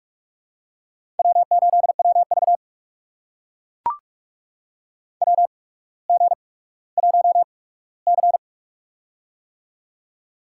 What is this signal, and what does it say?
Morse code, heard as a keyed tone, about 44 wpm, spaced as for 19 wpm, 690 Hz. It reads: W8WV WG1C
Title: USA Call Signs - 4 Characters - 8x Character Spacing - 45wpm